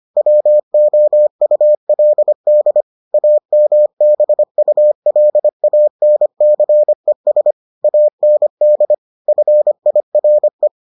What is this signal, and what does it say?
Morse code, heard as a keyed tone, about 25 wpm, 600 Hz. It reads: WOULD AMBULANCES AND FIRE